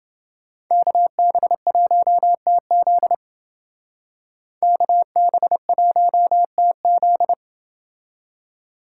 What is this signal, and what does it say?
Morse code, heard as a keyed tone, about 28 wpm, 700 Hz. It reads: KB1TZ KB1TZ